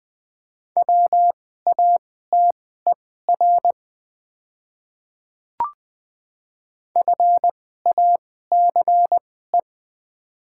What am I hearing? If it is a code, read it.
Morse code, heard as a keyed tone, about 20 wpm, spaced as for 15 wpm, 700 Hz. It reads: WATER FACE
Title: 2x Character Spacing - Top 500 Words - 20wpm